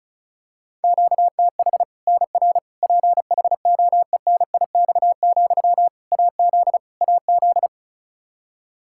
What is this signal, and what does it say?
Morse code, heard as a keyed tone, about 34 wpm, 700 Hz. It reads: QTH NR PHOENIX, AZ AZ
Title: K1USN SST QSO Practice 35wpm